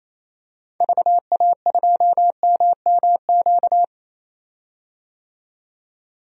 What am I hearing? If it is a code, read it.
Morse code, heard as a keyed tone, about 28 wpm, 700 Hz. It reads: VA2MMQ
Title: DXCC-100 Call Signs - 6 Characters - 28wpm